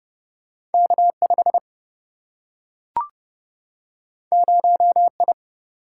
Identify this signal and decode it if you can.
Morse code, heard as a keyed tone, about 30 wpm, 700 Hz. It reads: K 5 0 I
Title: Single-Letter Single-Number 30wpm